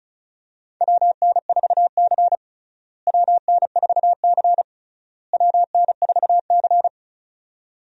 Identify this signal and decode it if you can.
Morse code, heard as a keyed tone, about 36 wpm, 700 Hz. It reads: WN4C WN4C WN4C
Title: USA Call Signs - 4 Characters - Repeated 3x - 35wpm